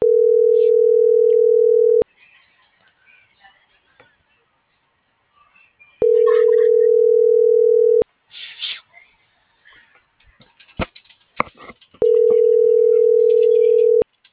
Ringback tone (North America), 440 + 480 Hz.